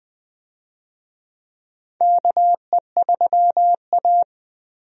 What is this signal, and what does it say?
Morse code, heard as a keyed tone, about 20 wpm, 700 Hz. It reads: KE3A
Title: USA Call Signs - 4 Characters - Repeated 2x - 20wpm